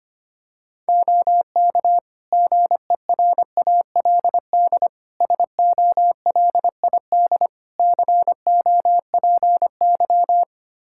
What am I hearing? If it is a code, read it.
Morse code, heard as a keyed tone, about 25 wpm, 700 Hz. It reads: OK GERALD SOLID COPY
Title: Practice QSO - Exchange 3 - More Info - 25wpm